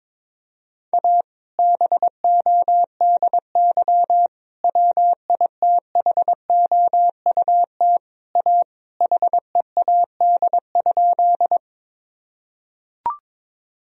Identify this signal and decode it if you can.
Morse code, heard as a keyed tone, about 22 wpm, 700 Hz. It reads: A BODY WITHOUT A HEAD?
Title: Sentences from Top 1000 Words - Review Part 3 - 22wpm